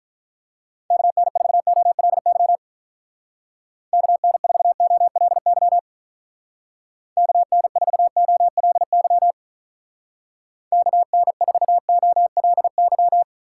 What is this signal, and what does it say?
Morse code, heard as a keyed tone, about 43 wpm, 700 Hz. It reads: KN4OLY KN4OLY KN4OLY KN4OLY
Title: Speed Racing - USA 2x3 Call Signs 35wpm